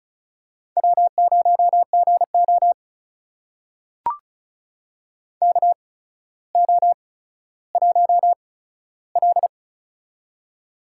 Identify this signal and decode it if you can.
Morse code, heard as a keyed tone, about 36 wpm, spaced as for 15 wpm, 700 Hz. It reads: W0GO KO1L
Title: USA Call Signs - 4 Characters - 8x Character Spacing - 35wpm